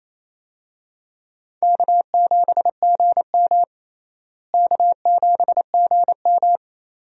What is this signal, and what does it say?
Morse code, heard as a keyed tone, about 28 wpm, 700 Hz. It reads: K7GM K7GM
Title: USA Call Signs - 4 Characters - Repeated 2x - 28wpm